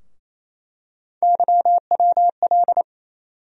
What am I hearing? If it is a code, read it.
Morse code, heard as a keyed tone, about 28 wpm, 700 Hz. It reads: YWL
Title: USA Call Sign Postfixes - 3 Letters 28wpm